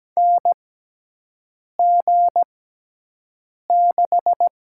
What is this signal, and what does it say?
Morse code, heard as a keyed tone, about 17 wpm, spaced as for 6 wpm, 700 Hz. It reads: NG6
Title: USA Call Signs - 4 Characters - 6x Character Spacing - 17wpm